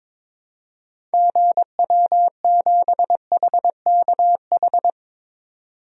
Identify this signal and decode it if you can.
Morse code, heard as a keyed tone, about 22 wpm, 700 Hz. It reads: GW7HKH
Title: DXCC-100 Call Signs - 6 Characters - 22wpm